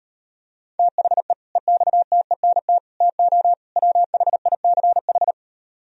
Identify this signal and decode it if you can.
Morse code, heard as a keyed tone, about 38 wpm, 700 Hz. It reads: THE EXTENT TO WHICH